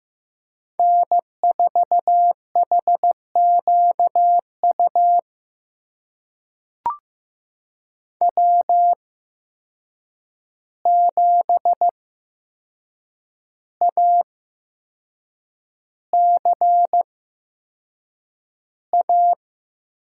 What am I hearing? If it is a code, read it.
Morse code, heard as a keyed tone, about 15 wpm, spaced as for 6 wpm, 700 Hz. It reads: N4HQU W7ACA
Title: USA Call Signs - 5 Characters - 8x Character Spacing - 15wpm